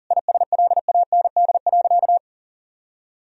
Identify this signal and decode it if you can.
Morse code, heard as a keyed tone, about 37 wpm, 700 Hz. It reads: ISLAND.
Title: Sentences from Top 1000 Words - Review Part 3 - 40wpm